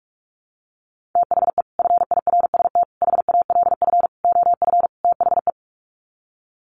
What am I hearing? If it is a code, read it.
Morse code, heard as a keyed tone, about 44 wpm, 700 Hz. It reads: THE FIRST HALF OF THE
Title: Sets of 5 Words 45wpm